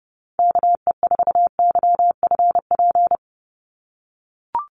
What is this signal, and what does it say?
Morse code, heard as a keyed tone, about 30 wpm, 700 Hz. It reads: KE4YFP
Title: USA Call Signs 30wpm